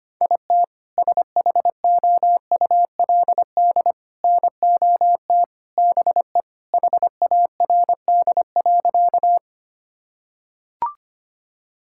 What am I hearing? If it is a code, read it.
Morse code, heard as a keyed tone, about 25 wpm, 690 Hz. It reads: IT SHOULD NOT BE HARD.
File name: practice / 25wpm / Sentences with 2-Word Phrases - 25wpm